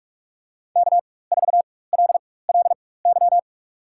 Morse code, heard as a keyed tone, about 46 wpm, spaced as for 21 wpm, 690 Hz. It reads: K4LLY
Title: USA Call Signs - 5 Characters - 4x Character Spacing - 45wpm